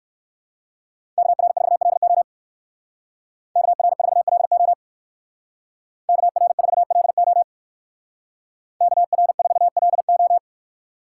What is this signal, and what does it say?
Morse code, heard as a keyed tone, about 58 wpm, 700 Hz. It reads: KR4LO KR4LO KR4LO KR4LO